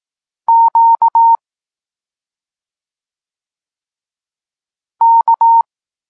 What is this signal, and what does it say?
Morse code, heard as a keyed tone, about 18 wpm, 920 Hz. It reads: Q K
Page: Tekst 32 – Doskonalenie odbioru poznanych liter